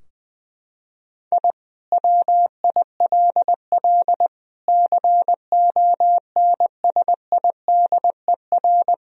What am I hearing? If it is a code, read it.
Morse code, heard as a keyed tone, about 21 wpm, 700 Hz. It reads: I WILL CONSIDER
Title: Sentences from Top 1000 Words - Review Part 4 - 20wpm